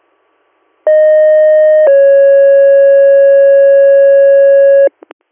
Two-tone page, heard as a one-second tone, then a three-second tone, about 620 then 550 Hz. It is heard from 0.9 to 4.9 s.